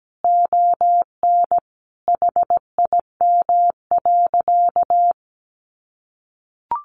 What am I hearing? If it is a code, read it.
Morse code, heard as a keyed tone, about 17 wpm, 700 Hz. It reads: ON HIM.
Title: Sentences from Top 1500 Words 17wpm